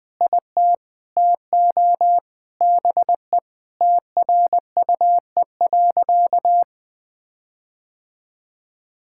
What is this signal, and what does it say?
Morse code, heard as a keyed tone, about 21 wpm, 700 Hz. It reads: IT TO BE TRUE.
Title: Sentences with 4-Word Phrases - Spoken Beginning - 20wpm